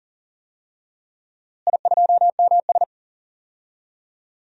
Morse code, heard as a keyed tone, about 40 wpm, 700 Hz. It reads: I2MS
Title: DXCC-100 Call Signs - 4 Characters - Repeated 3x - 40wpm